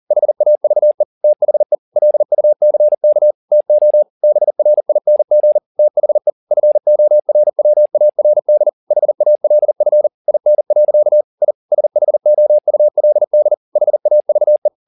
Morse code, heard as a keyed tone, about 40 wpm, 600 Hz. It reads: HAVE THE LUCK TO BRING THE FORWARD HALF IN. I SHOULD HAVE
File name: the old man and the sea 2 40wpm